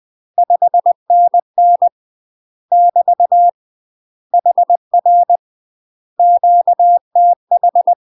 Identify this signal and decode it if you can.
Morse code, heard as a keyed tone, about 20 wpm, 700 Hz. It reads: 5NN <BT> HR QTH
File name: UK MASTER QSOs TRAINING / 20WPM / UK MASTER QSOs 0044